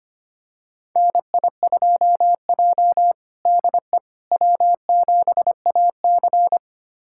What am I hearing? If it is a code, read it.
Morse code, heard as a keyed tone, about 25 wpm, 700 Hz. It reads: NI2J DE W7AC
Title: USA Callsign DE Callsign - 4 Characters - 25wpm